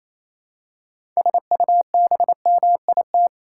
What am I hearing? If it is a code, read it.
Morse code, heard as a keyed tone, about 28 wpm, 700 Hz. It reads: SUBMIT
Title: Six Letter Words - 28wpm